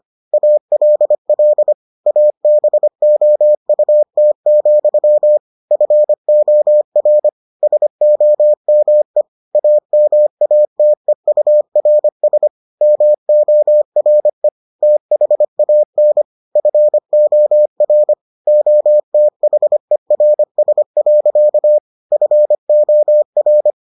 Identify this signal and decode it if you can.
Morse code, heard as a keyed tone, about 25 wpm, 600 Hz. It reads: ALL ABOUT, FOR SOME AMATEURS MORE THAN FOR OTHERS. FOR